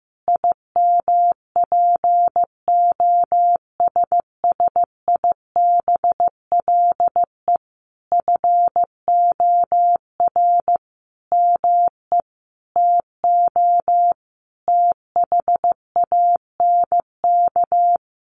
Morse code, heard as a keyed tone, about 15 wpm, 700 Hz. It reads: IMPOSSIBLE FOR ME TO THANK
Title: Sentences from Top 1100 Words 15wpm